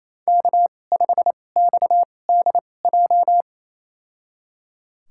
Morse code, heard as a keyed tone, about 28 wpm, spaced as for 20 wpm, 700 Hz. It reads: K5XDJ
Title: USA Call Signs - 5 Characters - 2x Character Spacing - 28wpm